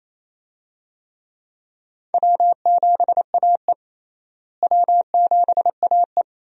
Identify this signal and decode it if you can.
Morse code, heard as a keyed tone, about 28 wpm, 700 Hz. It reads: W7AE W7AE
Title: USA Call Signs - 4 Characters - Repeated 2x - 28wpm